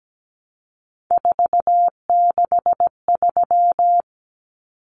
Morse code, heard as a keyed tone, about 17 wpm, 700 Hz. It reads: 463
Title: Common Cut Numbers - 3-Digit 17wpm